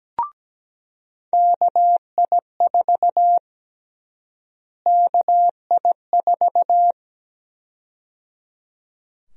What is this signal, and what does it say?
Morse code, heard as a keyed tone, about 17 wpm, 700 Hz. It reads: KI4 KI4
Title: USA Call Sign Prefixes - Repeated 2x - 17wpm